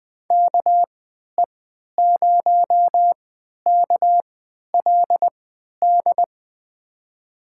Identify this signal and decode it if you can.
Morse code, heard as a keyed tone, about 20 wpm, spaced as for 11 wpm, 700 Hz. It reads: KE0KLD
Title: USA Call Signs - 6 Characters - 3x Character Spacing - 20wpm